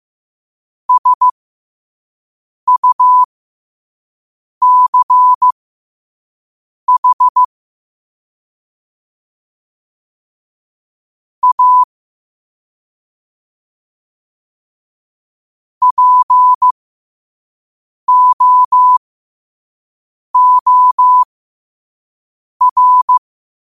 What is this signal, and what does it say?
Morse code, heard as a keyed tone, about 15 wpm, spaced as for 5 wpm, 1000 Hz. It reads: SUCH A POOR
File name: qotdfiles / 5wpm